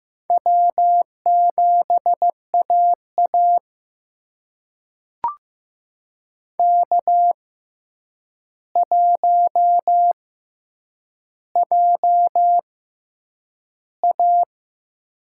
Morse code, heard as a keyed tone, about 15 wpm, spaced as for 8 wpm, 700 Hz. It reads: W7AA K1JA
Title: USA Call Signs - 4 Characters - 6x Character Spacing - 15wpm